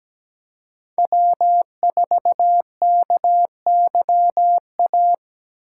Morse code, heard as a keyed tone, about 17 wpm, 700 Hz. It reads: W4KYA
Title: USA Call Signs - 2x Character Spacing - 17wpm